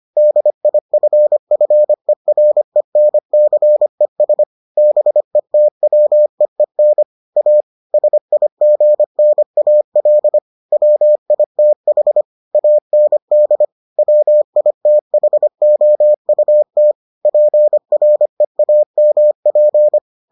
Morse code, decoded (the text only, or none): DIFFERENCES BETWEEN A SIGNAL WITH AND WITHOUT PREAMP